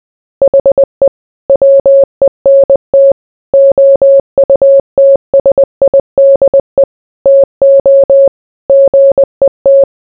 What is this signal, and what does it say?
Morse code, heard as a keyed tone, about 20 wpm, 550 Hz. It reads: HE WENT OUTSIDE TO GET